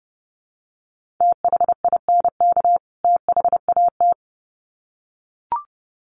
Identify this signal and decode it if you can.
Morse code, heard as a keyed tone, about 30 wpm, 700 Hz. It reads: THINK THAT
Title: Sets of 2 Words 30wpm